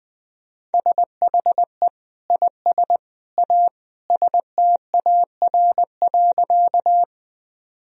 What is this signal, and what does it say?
Morse code, heard as a keyed tone, about 20 wpm, 700 Hz. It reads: SHE IS A STAR.
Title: Sentences from Top 1000 Words - Review Part 2 - 20wpm